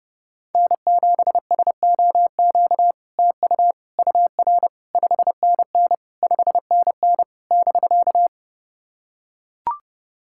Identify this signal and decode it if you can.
Morse code, heard as a keyed tone, about 31 wpm, 700 Hz. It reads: N7SOQ TU UR 5NN 5NN <BK>
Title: SOTA QSO Practice 30wpm